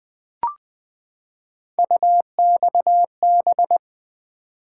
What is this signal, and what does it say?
Morse code, heard as a keyed tone, about 20 wpm, 700 Hz. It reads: UXB
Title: Three Letter-Number Combinations - 20wpm